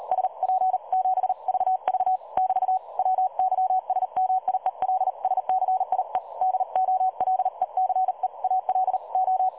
Morse code, heard as a keyed tone, about 38 wpm, 750 Hz. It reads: SP7VV <BT> WYSMIELSCIE DOLECEALO
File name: dlazaawansowanych / nagraniazpasma / websdr recording 3565.0kHz